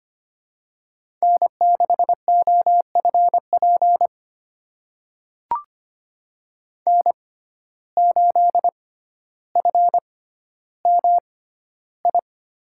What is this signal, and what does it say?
Morse code, heard as a keyed tone, about 25 wpm, spaced as for 13 wpm, 700 Hz. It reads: N6OFP N8FMI
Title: USA Call Signs - 5 Characters - 6x Character Spacing - 25wpm